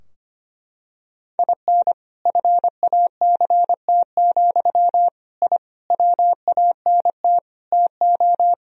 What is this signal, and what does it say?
Morse code, heard as a keyed tone, about 25 wpm, 700 Hz. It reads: IN FACT, I WANT TO
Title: Sentences from Top 1000 Words - Review Part 3 - 25wpm